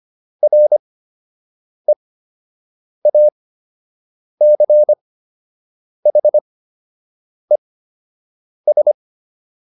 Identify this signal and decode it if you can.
Morse code, heard as a keyed tone, about 25 wpm, spaced as for 7 wpm, 600 Hz. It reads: REACHES